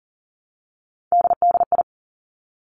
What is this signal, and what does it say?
Morse code, heard as a keyed tone, about 38 wpm, 700 Hz. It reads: DDI
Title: Three Challenging Letter-Number Combinations 40wpm